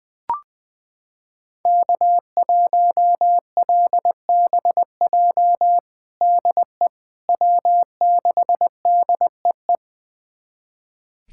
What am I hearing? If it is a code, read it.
Morse code, heard as a keyed tone, about 21 wpm, 700 Hz. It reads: K1LBJ DE W6DEE
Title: USA Callsign DE Callsign - 5 Characters - 20wpm